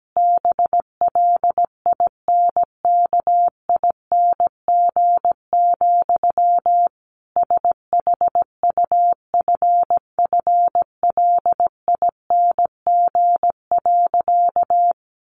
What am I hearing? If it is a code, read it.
Morse code, heard as a keyed tone, about 17 wpm, 700 Hz. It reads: BLINKING, SHUFFLING.
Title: A Sound Of Thunder 17wpm